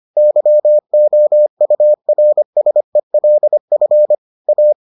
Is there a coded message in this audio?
Yes: Morse code from 0.2 to 4.7 s, heard as a keyed tone, about 25 wpm, 600 Hz. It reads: YOURSELF A